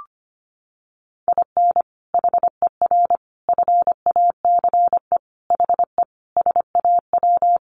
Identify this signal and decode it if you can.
Morse code, heard as a keyed tone, about 25 wpm, 700 Hz. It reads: IN HER FACE HE SAW
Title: Sentences from Top 1300 Words 25wpm